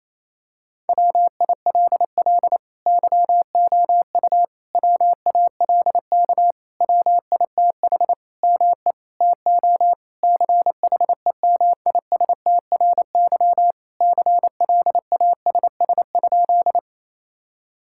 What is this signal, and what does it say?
Morse code, heard as a keyed tone, about 28 wpm, 700 Hz. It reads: WILL YOU WALK WITH ME TO CHEMISTRY CLASS?
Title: Sentences from Top 1000 Words - Review Part 6 - 28wpm